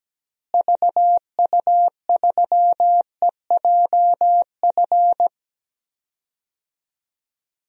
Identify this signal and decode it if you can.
Morse code, heard as a keyed tone, about 17 wpm, 700 Hz. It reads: VU3EJF
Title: DXCC-100 Call Signs - 6 Characters - Repeated 2x - 17wpm